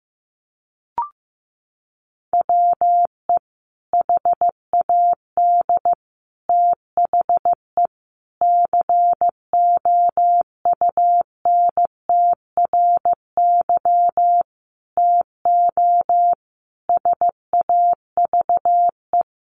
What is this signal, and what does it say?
Morse code, heard as a keyed tone, about 15 wpm, 700 Hz. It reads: WE HAD THE COUNTRY TO SAVE
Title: Sentences from Top 1100 Words 15wpm